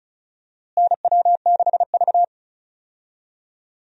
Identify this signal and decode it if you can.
Morse code, heard as a keyed tone, about 36 wpm, 700 Hz. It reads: NW6V
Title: USA Call Signs - 4 Characters - Repeated 3x - 35wpm